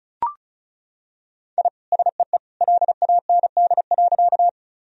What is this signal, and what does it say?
Morse code, heard as a keyed tone, about 36 wpm, 700 Hz. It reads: I SEE LAND.
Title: Sentences from Top 1000 Words - Review Part 1 - 35wpm